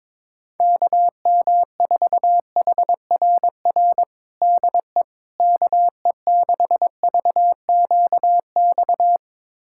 Morse code, heard as a keyed tone, about 22 wpm, 700 Hz. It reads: KM4HRR DE KE6VQX
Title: USA Callsign DE Callsign 22wpm